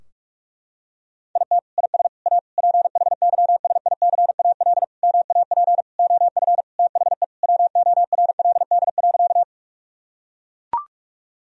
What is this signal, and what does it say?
Morse code, heard as a keyed tone, about 43 wpm, 700 Hz. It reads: IT IS A PHYSICAL MAP OF THE WORLD.
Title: Sentences from Top 1000 Words - Review Part 4 - 45wpm